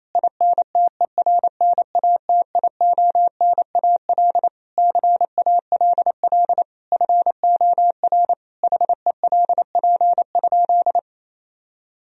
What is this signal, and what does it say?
Morse code, heard as a keyed tone, about 28 wpm, 700 Hz. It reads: INTERNATIONAL CALL FOR HELP?
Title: Sentences from Top 1000 Words - Review Part 4 - 28wpm